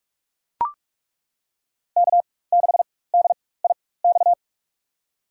Morse code, heard as a keyed tone, about 42 wpm, spaced as for 21 wpm, 700 Hz. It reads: K6DIX
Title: USA Call Signs - 5 Characters - 4x Character Spacing - 45wpm